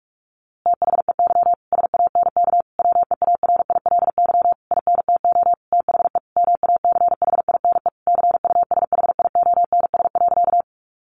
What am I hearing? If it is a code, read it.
Morse code, heard as a keyed tone, about 44 wpm, 700 Hz. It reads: THEY SANK WEARILY INTO THE MACHINE CUSHIONS.
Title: A Sound Of Thunder 45wpm